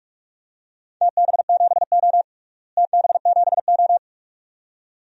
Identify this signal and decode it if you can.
Morse code, heard as a keyed tone, about 46 wpm, 700 Hz. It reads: TB7O TB7O
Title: DXCC-100 Call Signs - 4 Characters - Repeated 2x - 45wpm